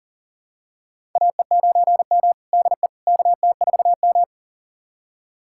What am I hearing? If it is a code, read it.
Morse code, heard as a keyed tone, about 37 wpm, 700 Hz. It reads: AE9M DE KT4M
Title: USA Callsign DE Callsign - 4 Characters - 40wpm